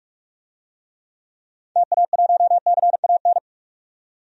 Morse code, heard as a keyed tone, about 47 wpm, 700 Hz. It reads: TA1CAN
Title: DXCC-100 Call Signs - 6 Characters - 45wpm